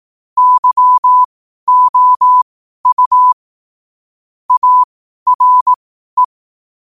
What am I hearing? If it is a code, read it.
Morse code, heard as a keyed tone, about 18 wpm, spaced as for 12 wpm, 1000 Hz. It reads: YOU ARE
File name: qotdfiles / 12wpm